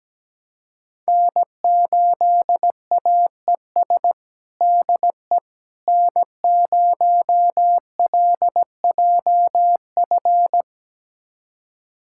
Morse code, heard as a keyed tone, about 17 wpm, 700 Hz. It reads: N8AES DE N0LJF
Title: USA Callsign DE Callsign - 5 Characters - 17wpm